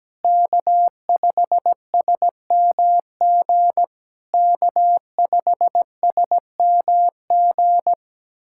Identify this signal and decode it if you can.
Morse code, heard as a keyed tone, about 17 wpm, 700 Hz. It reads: K5SMG K5SMG
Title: SOTA QSO Practice 17wpm